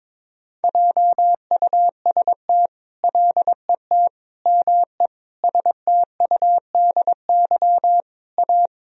Morse code, heard as a keyed tone, about 22 wpm, 700 Hz. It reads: JUST LET ME STUDY A
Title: Sentences from Top 1000 Words - Review Part 1 - 22wpm